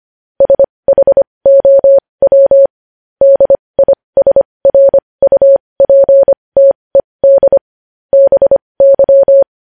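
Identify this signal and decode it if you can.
Morse code, heard as a keyed tone, about 25 wpm, spaced as for 20 wpm, 550 Hz. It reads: SHOW DISRUPTED BY
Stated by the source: Morse news headlines sent at 20, opening with vvv.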